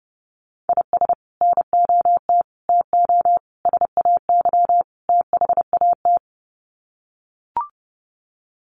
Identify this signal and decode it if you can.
Morse code, heard as a keyed tone, about 29 wpm, 700 Hz. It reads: IS NOT TO SAY THAT
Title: Sets of 5 Words 30wpm